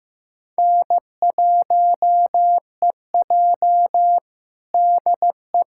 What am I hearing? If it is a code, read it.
Morse code, heard as a keyed tone, about 15 wpm, 700 Hz. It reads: N1EJ DE
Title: USA Callsign DE Callsign - 4 Characters - 15wpm